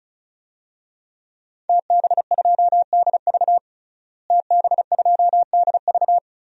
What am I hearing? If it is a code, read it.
Morse code, heard as a keyed tone, about 34 wpm, 700 Hz. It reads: TB2DV TB2DV
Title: DXCC-100 Call Signs - 5 Characters - Repeated 2x - 35wpm